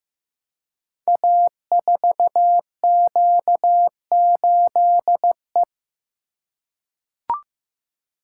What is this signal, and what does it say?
Morse code, heard as a keyed tone, about 15 wpm, 700 Hz. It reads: A4Q8E
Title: Code Groups - 5 Letter:Numbers - 15wpm